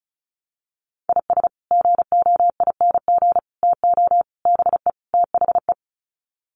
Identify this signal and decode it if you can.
Morse code, heard as a keyed tone, about 34 wpm, 700 Hz. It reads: IS GOING TO BE THE
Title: Sets of 5 Words 35wpm